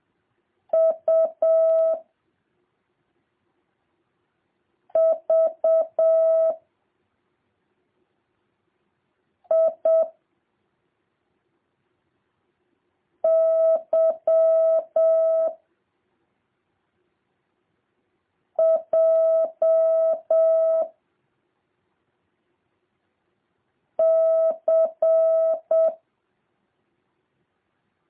Morse code, heard as a keyed tone, about 7 wpm, 650 Hz. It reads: U V I Y J C